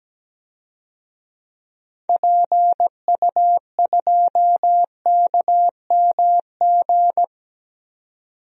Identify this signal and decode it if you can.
Morse code, heard as a keyed tone, about 17 wpm, 700 Hz. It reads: PU2KMG
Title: DXCC-100 Call Signs - 6 Characters - 17wpm